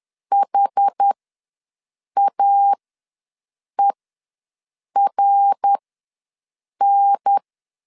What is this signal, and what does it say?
Morse code, heard as a keyed tone, about 10 wpm, spaced as for 6 wpm, 800 Hz. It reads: HAERN
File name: LEZIONI / 8 YJ45 / 8 YJ4 ws 10wpm 4020